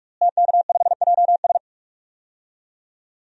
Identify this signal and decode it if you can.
Morse code, heard as a keyed tone, about 44 wpm, 700 Hz. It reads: TK5JS
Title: DXCC-100 Call Signs - 5 Characters - Repeated 3x - 45wpm